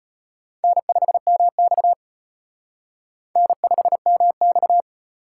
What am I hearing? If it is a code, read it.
Morse code, heard as a keyed tone, about 31 wpm, 700 Hz. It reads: N5MX N5MX
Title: Speed Racing - USA 1x2 Call Signs 25wpm